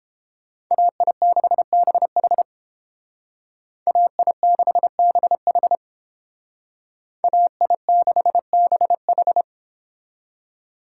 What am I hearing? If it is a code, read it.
Morse code, heard as a keyed tone, about 30 wpm, 700 Hz. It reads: AI6BH AI6BH AI6BH